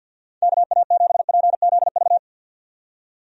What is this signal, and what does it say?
Morse code, heard as a keyed tone, about 49 wpm, 700 Hz. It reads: KA7PZV
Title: USA Call Signs - 6 Characters - 50wpm